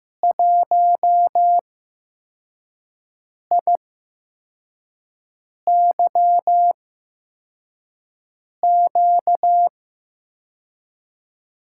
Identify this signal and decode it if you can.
Morse code, heard as a keyed tone, about 15 wpm, spaced as for 4 wpm, 700 Hz. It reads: 1IYQ
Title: USA Call Signs - 5 Characters - 8x Character Spacing - 15wpm